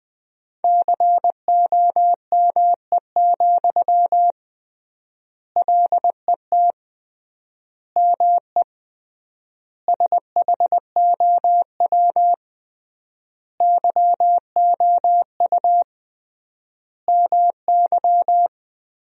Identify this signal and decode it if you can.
Morse code, heard as a keyed tone, about 20 wpm, 700 Hz. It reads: COME, LET ME SHOW YOU MY
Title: 3x Word-Spacing - Sentences from Top 500 Words - 20wpm